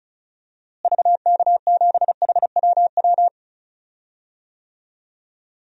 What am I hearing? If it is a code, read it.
Morse code, heard as a keyed tone, about 36 wpm, 700 Hz. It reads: VK7HWW
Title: DXCC-100 Call Signs - 6 Characters - Repeated 3x - 35wpm